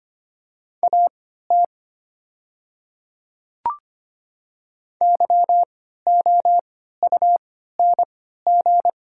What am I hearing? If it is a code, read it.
Morse code, heard as a keyed tone, about 25 wpm, spaced as for 14 wpm, 690 Hz. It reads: AT YOUNG
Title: 3x Character Spacing - Top 500 Words - 25wpm